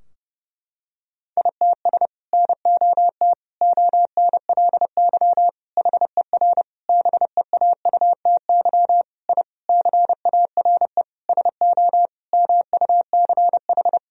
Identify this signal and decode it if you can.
Morse code, heard as a keyed tone, about 32 wpm, 700 Hz. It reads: ITS NOT ONLY HER BEAUTY I CARE SO MUCH
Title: Sentences from Top 1000 Words - Review Part 2 - 30wpm